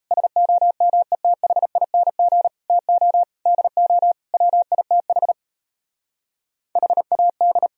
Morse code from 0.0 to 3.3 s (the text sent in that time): SOMETHING TO